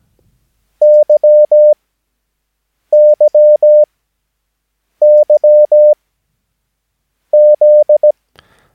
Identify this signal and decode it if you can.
Morse code, heard as a keyed tone, about 17 wpm, spaced as for 6 wpm, 600 Hz. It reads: YYYZ
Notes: Övningsmaterial med alfabetet A till Ö samt siffror 0 till 9 enligt International/Continental Morse Code.